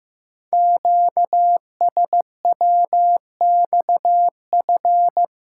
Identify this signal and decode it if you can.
Morse code, heard as a keyed tone, about 15 wpm, 700 Hz. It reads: QSWXF
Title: Code Groups - 5 Letters - 15wpm